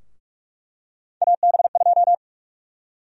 Morse code, heard as a keyed tone, about 46 wpm, 700 Hz. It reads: AB2
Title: USA Call Sign Prefixes - Repeated 2x - 45wpm